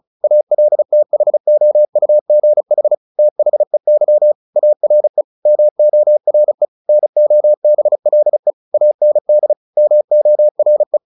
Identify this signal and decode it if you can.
Morse code, heard as a keyed tone, about 35 wpm, 600 Hz. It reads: ALTHOUGH THEY ARE MORE NOBLE AND MORE
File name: the old man and the sea / the old man and the sea 1 35wpm